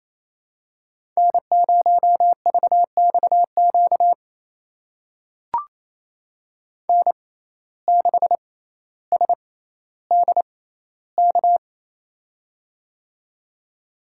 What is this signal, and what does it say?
Morse code, heard as a keyed tone, about 28 wpm, spaced as for 14 wpm, 700 Hz. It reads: N0VXQ N6SDK
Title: USA Call Signs - 6x Character Spacing - 28wpm